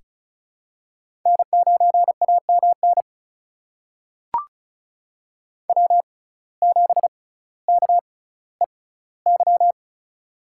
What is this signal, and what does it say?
Morse code, heard as a keyed tone, about 34 wpm, spaced as for 18 wpm, 700 Hz. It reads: N9AMN W7KEY
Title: USA Call Signs - 5 Characters - 6x Character Spacing - 35wpm